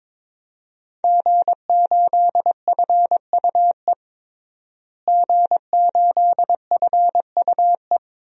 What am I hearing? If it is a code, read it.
Morse code, heard as a keyed tone, about 22 wpm, 700 Hz. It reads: G8FUE G8FUE
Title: DXCC-100 Call Signs - 5 Characters - Repeated 3x - 22wpm